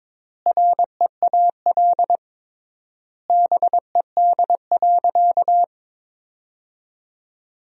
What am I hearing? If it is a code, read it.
Morse code, heard as a keyed tone, about 22 wpm, 700 Hz. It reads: REAL BED.
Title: 3x Word-Spacing - Sentences from Top 500 Words - 22wpm